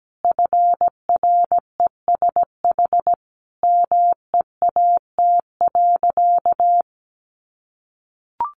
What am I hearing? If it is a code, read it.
Morse code, heard as a keyed tone, about 17 wpm, 700 Hz. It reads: FRESH MEAT.
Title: Sentences from Top 1200 Words 17wpm